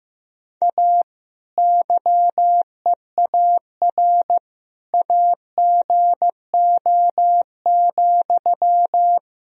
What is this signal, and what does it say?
Morse code, heard as a keyed tone, about 15 wpm, 700 Hz. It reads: A YEAR AGO,
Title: Sentences from Top 1000 Words - Review Part 3 - 15wpm